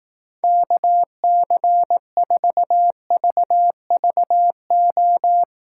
Morse code, heard as a keyed tone, about 18 wpm, 700 Hz. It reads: KC4VVO
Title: Speed Racing - USA 2x3 Call Signs 15wpm